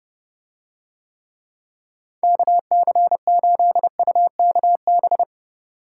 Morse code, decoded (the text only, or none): KC8UKB